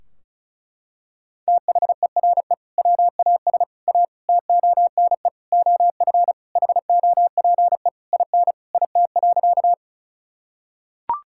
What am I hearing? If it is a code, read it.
Morse code, heard as a keyed tone, about 34 wpm, 700 Hz. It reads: THERE WAS A TONE OF HOPE IN IT.
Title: Sentences from Top 1600 Words 35wpm